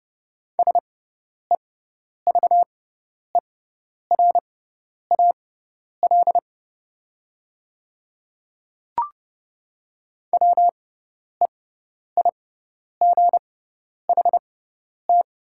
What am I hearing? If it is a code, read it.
Morse code, heard as a keyed tone, about 30 wpm, spaced as for 10 wpm, 700 Hz. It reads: SEVERAL WEIGHT
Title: 6x Character Spacing - Top 500 Words - 30wpm